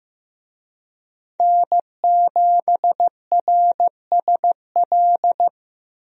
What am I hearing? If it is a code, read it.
Morse code, heard as a keyed tone, about 15 wpm, 700 Hz. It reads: N7RSL
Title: USA Call Signs - 5 Characters - Repeated 3x - 15wpm